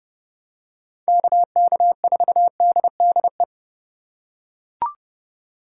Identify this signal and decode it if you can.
Morse code, heard as a keyed tone, about 30 wpm, 700 Hz. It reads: KK4DDE
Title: USA Call Signs - 6 Characters - 4x Character Spacing - 30wpm